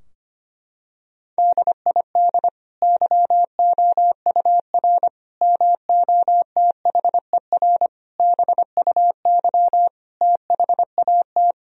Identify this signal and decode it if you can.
Morse code, heard as a keyed tone, about 25 wpm, 700 Hz. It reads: DID YOUR MOTHER BUY THAT
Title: Sentences from Top 1000 Words - Review Part 3 - 25wpm